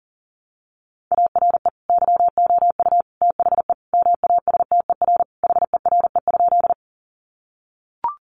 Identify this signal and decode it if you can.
Morse code, heard as a keyed tone, about 40 wpm, 700 Hz. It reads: ARE YOU THE MASTER HERE?
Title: Sentences from Top 1300 Words 40wpm